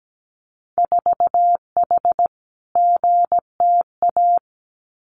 Morse code, heard as a keyed tone, about 17 wpm, 700 Hz. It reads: 4H GTA
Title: Winter Field Day Exchanges Encore 17wpm